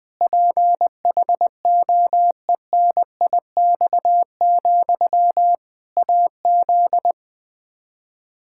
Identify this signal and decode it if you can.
Morse code, heard as a keyed tone, about 20 wpm, 700 Hz. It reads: PHOENIX, AZ
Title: Practice QSO - Exchange 1 - Beginning - 20wpm